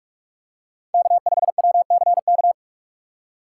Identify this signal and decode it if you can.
Morse code, heard as a keyed tone, about 48 wpm, 700 Hz. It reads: K5WCK
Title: USA Call Signs - 5 Characters - 2x Character Spacing - 45wpm